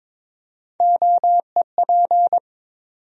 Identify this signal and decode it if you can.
Morse code, heard as a keyed tone, about 22 wpm, 700 Hz. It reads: OEP
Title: USA Call Sign Postfixes - 3 Letters 22wpm